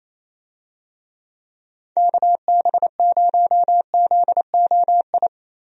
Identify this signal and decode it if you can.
Morse code, heard as a keyed tone, about 27 wpm, 700 Hz. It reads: KB0ZOI
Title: USA Call Signs - 6 Characters - Repeated 2x - 28wpm